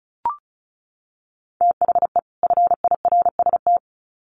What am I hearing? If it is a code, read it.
Morse code, heard as a keyed tone, about 34 wpm, 700 Hz. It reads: THE FIRST
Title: Sets of 2 Words 35wpm